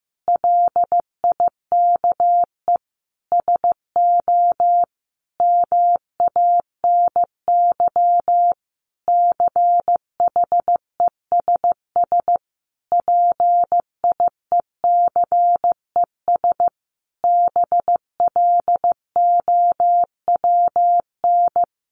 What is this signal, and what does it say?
Morse code, heard as a keyed tone, about 15 wpm, 700 Hz. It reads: LIKE SO MANY CHESS PIECES BLOWN
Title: A Sound Of Thunder 15wpm